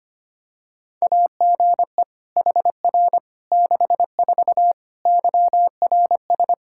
Morse code, heard as a keyed tone, about 25 wpm, 700 Hz. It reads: AGE HR 64 YRS
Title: K1USN SST QSO Practice 25wpm